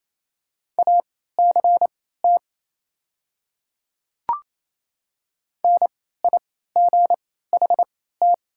Morse code, heard as a keyed tone, about 27 wpm, spaced as for 16 wpm, 690 Hz. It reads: ACT NIGHT
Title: 3x Character Spacing - Top 300 Words - 28wpm